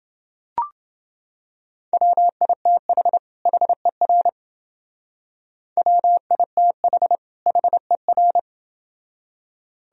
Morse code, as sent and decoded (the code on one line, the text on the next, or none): .-- .. - .... / .... .-. / .-- .. - .... / .... .-.
WITH HER WITH HER